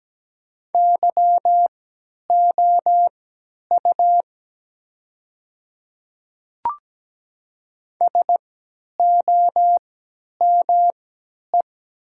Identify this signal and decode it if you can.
Morse code, heard as a keyed tone, about 17 wpm, spaced as for 10 wpm, 700 Hz. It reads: YOU SOME
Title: 3x Character Spacing - Top 100 Words - 17wpm